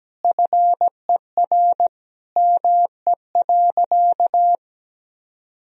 Morse code, heard as a keyed tone, about 17 wpm, 700 Hz. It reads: FER ME.
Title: Sentences from Top 300 Words and Abbreviations 17wpm